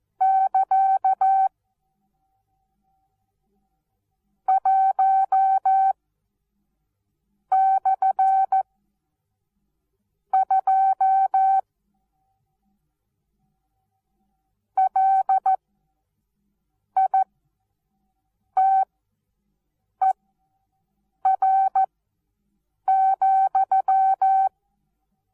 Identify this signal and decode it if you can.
Morse code, heard as a keyed tone, about 14 wpm, spaced as for 5 wpm, 770 Hz. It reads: <KA> 1/2 LITER,
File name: Onlinelehrgang / morsen / DJ4UF Morsen19